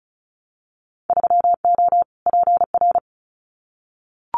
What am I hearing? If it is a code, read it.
Morse code, heard as a keyed tone, about 36 wpm, 700 Hz. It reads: 3O PR
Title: Winter Field Day Exchanges Encore 35wpm